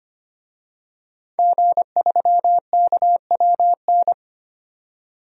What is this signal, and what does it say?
Morse code, heard as a keyed tone, about 25 wpm, 700 Hz. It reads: G3KWN
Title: DXCC-100 Call Signs - 5 Characters - 25wpm